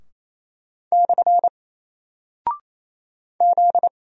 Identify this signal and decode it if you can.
Morse code, heard as a keyed tone, about 28 wpm, 700 Hz. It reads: / Z
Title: Single Character (Letter:Number:Punctuation:Prosign) Alt 28wpm